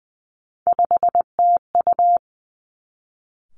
Morse code, heard as a keyed tone, about 20 wpm, 700 Hz. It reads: 5TU
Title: Three Challenging Letter-Number Combinations 20wpm